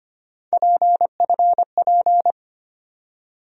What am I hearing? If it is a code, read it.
Morse code, heard as a keyed tone, about 25 wpm, 700 Hz. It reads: PFP
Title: USA Call Sign Postfixes - 3 Letters 25wpm